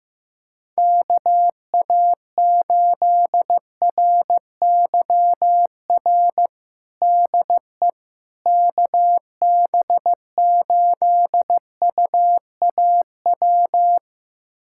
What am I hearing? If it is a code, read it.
Morse code, heard as a keyed tone, about 15 wpm, 700 Hz. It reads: KA8RYR DE KB8UAW
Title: USA Callsign DE Callsign 15wpm